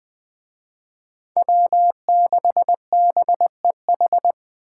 Morse code, heard as a keyed tone, about 25 wpm, 700 Hz. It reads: W6BEH
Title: Speed Racing - USA 1x3 Call Signs 20wpm